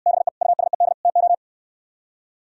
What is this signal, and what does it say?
Morse code, heard as a keyed tone, about 69 wpm, 700 Hz. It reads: BE USED TO